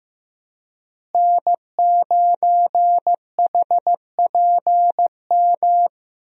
Morse code, heard as a keyed tone, about 15 wpm, 700 Hz. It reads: N9HPM
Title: USA Call Signs - 5 Characters - Repeated 3x - 15wpm